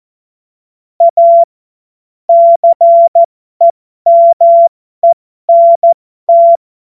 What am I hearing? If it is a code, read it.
Morse code, heard as a keyed tone, about 14 wpm, spaced as for 12 wpm, 680 Hz. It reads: A CEMENT
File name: Morsekurs / Witze